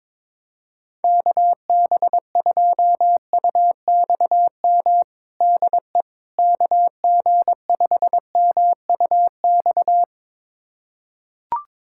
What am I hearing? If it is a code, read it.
Morse code, heard as a keyed tone, about 22 wpm, 700 Hz. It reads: KB2UXM DE KG5MUX
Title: USA Callsign DE Callsign - 6 Characters - 22wpm